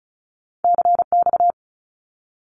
Morse code, heard as a keyed tone, about 36 wpm, 700 Hz. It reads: CX
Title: Abbreviations 35wpm